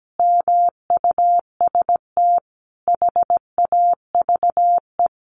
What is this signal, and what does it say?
Morse code, heard as a keyed tone, about 17 wpm, 700 Hz. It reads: MUST HAVE
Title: Sentences from Top 1100 Words 17wpm